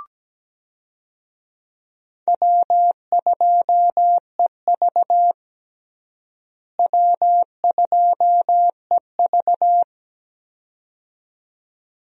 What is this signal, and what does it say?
Morse code, heard as a keyed tone, about 17 wpm, 700 Hz. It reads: W2EV W2EV
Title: USA Call Signs - 4 Characters - Repeated 2x - 17wpm